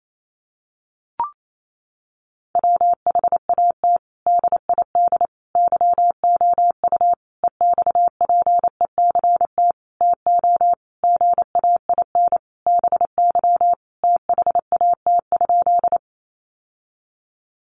Morse code, heard as a keyed tone, about 28 wpm, 700 Hz. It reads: WHAT DID YOU EXPECT TO GAIN BY THAT?
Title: Sentences from Top 1100 Words 28wpm